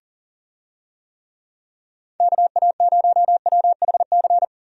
Morse code, heard as a keyed tone, about 40 wpm, 700 Hz. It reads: KA0WHC
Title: USA Call Signs - 6 Characters - Repeated 2x - 40wpm